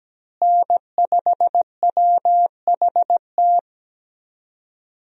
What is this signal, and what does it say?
Morse code, heard as a keyed tone, about 17 wpm, 700 Hz. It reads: N5WHT
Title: USA Call Signs - 5 Characters - Repeated 2x - 17wpm